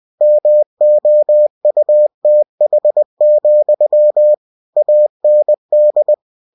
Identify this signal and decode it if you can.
Morse code, heard as a keyed tone, about 20 wpm, 600 Hz. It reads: MOUTH, AND
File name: cw-books / winnie-the-pooh / winnie-20wpm-page-0005